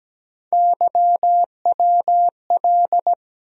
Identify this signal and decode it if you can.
Morse code, heard as a keyed tone, about 17 wpm, 700 Hz. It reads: YWL
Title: USA Call Sign Postfixes - 3 Letters 17wpm